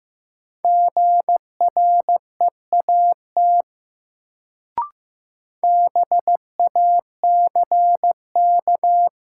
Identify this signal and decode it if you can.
Morse code, heard as a keyed tone, about 16 wpm, 700 Hz. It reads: GREAT BACK
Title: Sentences from Top 200 Words - New Word Drill - 15wpm